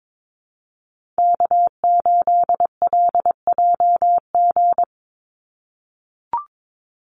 Morse code, heard as a keyed tone, about 22 wpm, 700 Hz. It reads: K8LJG
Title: NA SOTA Chaser Callsigns 22wpm